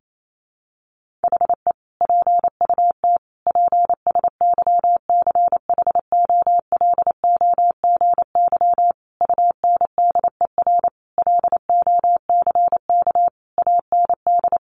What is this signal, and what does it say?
Morse code, heard as a keyed tone, about 28 wpm, 700 Hz. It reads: HE PUT PSYCHOLOGY UNDER LOCK AND
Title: Sentences from Top 1400 Words 28wpm